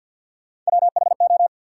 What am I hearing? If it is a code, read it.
Morse code, heard as a keyed tone, about 53 wpm, 700 Hz. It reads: WHO
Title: Sets of 2 Words - New Word Drill - 50wpm